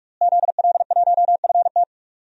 Morse code, heard as a keyed tone, about 45 wpm, 700 Hz. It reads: ZL1FT
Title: DXCC-100 Call Signs - 5 Characters - Repeated 2x - 45wpm